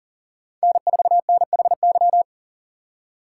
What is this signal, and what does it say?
Morse code, heard as a keyed tone, about 47 wpm, 700 Hz. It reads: N4NHY
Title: Speed Racing - USA 1x3 Call Signs 40wpm